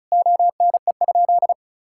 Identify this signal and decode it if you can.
Morse code, heard as a keyed tone, about 36 wpm, 690 Hz. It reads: ONE?
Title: Sentences from Top 1000 Words - Review Part 6 - 35wpm